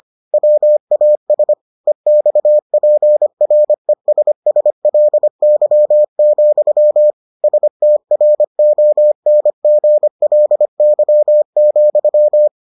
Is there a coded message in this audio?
Yes: Morse code from 0.3 to 12.5 s, heard as a keyed tone, about 25 wpm, 600 Hz. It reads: WAS EXPRESSLY, STRONGLY,